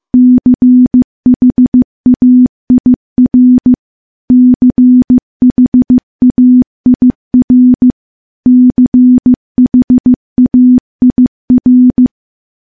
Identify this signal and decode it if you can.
Morse code, heard as a keyed tone, about 15 wpm, 250 Hz. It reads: CHAIR CHAIR CHAIR
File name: morse-training-15WPM